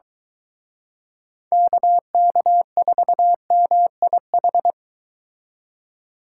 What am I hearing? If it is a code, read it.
Morse code, heard as a keyed tone, about 23 wpm, 700 Hz. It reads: KK4MIH